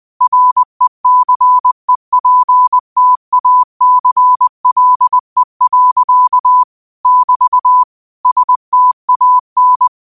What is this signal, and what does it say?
Morse code, heard as a keyed tone, about 20 wpm, 1000 Hz. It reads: RECEPTACLE. <BT> STAN
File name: qotdfiles / 20wpm